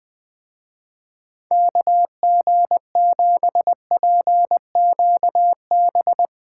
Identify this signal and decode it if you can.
Morse code, heard as a keyed tone, about 20 wpm, 700 Hz. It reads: KG7PQB
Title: Speed Racing - USA 2x3 Call Signs 17wpm